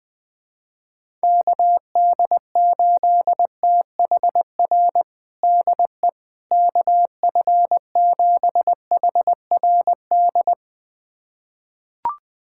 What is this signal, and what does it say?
Morse code, heard as a keyed tone, about 20 wpm, 700 Hz. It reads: KD8THR DE KF7HRD
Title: USA Callsign DE Callsign - 6 Characters - 20wpm